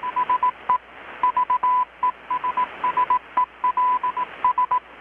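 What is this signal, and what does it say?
Morse code, heard as a keyed tone, about 18 wpm, 1000 Hz, with radio noise. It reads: SE VESSELS
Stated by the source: Morse from KPH's Night of Nights XX maritime radiotelegraphy event, received in Chiapas, Mexico about 3,800 km away on a Kenwood R-600 with a grounded 10 m long-wire antenna.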